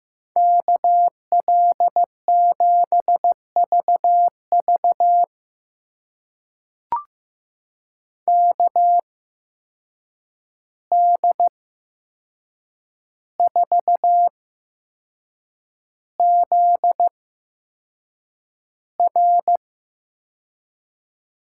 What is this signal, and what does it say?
Morse code, heard as a keyed tone, about 15 wpm, spaced as for 6 wpm, 700 Hz. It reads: KL7VV KD4ZR
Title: USA Call Signs - 8x Character Spacing - 15wpm